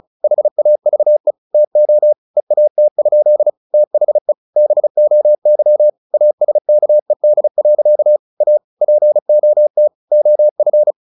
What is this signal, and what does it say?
Morse code, heard as a keyed tone, about 34 wpm, 600 Hz. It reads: HAVE TO EAT? THE BOY ASKED. A POT OF
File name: the old man and the sea / the old man and the sea 1 35wpm